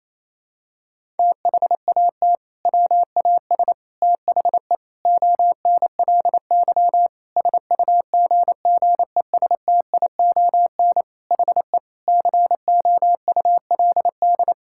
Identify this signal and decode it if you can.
Morse code, heard as a keyed tone, about 27 wpm, 700 Hz. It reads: THAT WAS THE ONLY SUGGESTION HE COULD
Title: Sentences from Top 1600 Words 28wpm